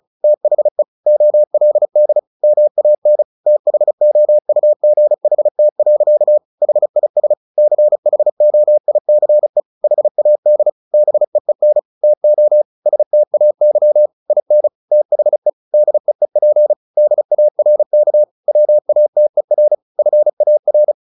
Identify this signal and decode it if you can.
Morse code, heard as a keyed tone, about 34 wpm, 600 Hz. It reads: THE OLD MAN THOUGHT. HIS CHOICE HAD BEEN TO STAY IN THE DEEP DARK WATER FAR
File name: the old man and the sea / the old man and the sea 1 35wpm